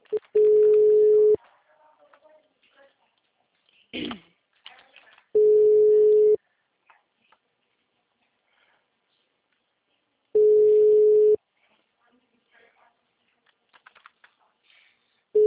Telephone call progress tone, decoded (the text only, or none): ringback tone